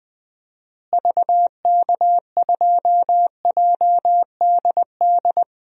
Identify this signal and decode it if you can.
Morse code, heard as a keyed tone, about 20 wpm, 700 Hz. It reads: VK2JDD
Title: DXCC-100 Call Signs - 6 Characters - Repeated 2x - 20wpm